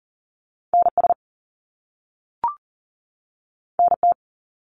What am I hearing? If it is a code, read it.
Morse code, heard as a keyed tone, about 40 wpm, 700 Hz. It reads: NS NT
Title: Canadian Provinces and Territories 40wpm